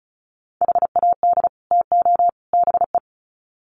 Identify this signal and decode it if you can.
Morse code, heard as a keyed tone, about 34 wpm, 700 Hz. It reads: HAD TO BE
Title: Sets of 3 Words 35wpm